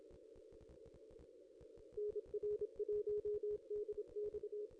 Morse code, heard as a keyed tone, about 26 wpm, 420 Hz. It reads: NR1DX
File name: contests / ss-cw-k5tr-op-n5rz -2200